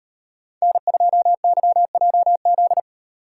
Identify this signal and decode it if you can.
Morse code, heard as a keyed tone, about 35 wpm, 700 Hz. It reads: N2YJZ